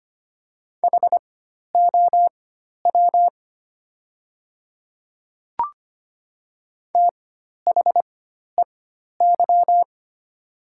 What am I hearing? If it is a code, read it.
Morse code, heard as a keyed tone, about 25 wpm, spaced as for 12 wpm, 690 Hz. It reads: HOW THEY
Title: 4x Character Spacing - Top 100 Words - 25wpm